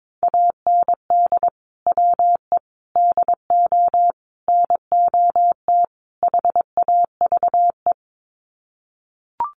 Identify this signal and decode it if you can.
Morse code, heard as a keyed tone, about 22 wpm, 700 Hz. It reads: AND WE DO NOT HAVE
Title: Sets of 5 Words Encore 22wpm